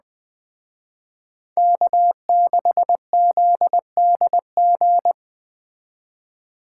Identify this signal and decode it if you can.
Morse code, heard as a keyed tone, about 21 wpm, 700 Hz. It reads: K6ZDG
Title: Speed Racing - USA 1x3 Call Signs 17wpm